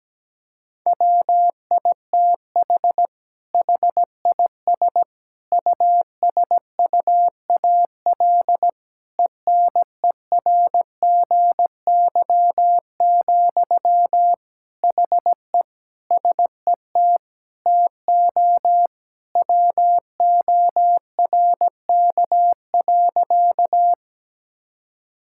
Morse code, heard as a keyed tone, about 17 wpm, 700 Hz. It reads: WITH HIS USUAL ENERGY, HE SET TO WORK.
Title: Sentences from Top 1000 Words - Review Part 6 - 17wpm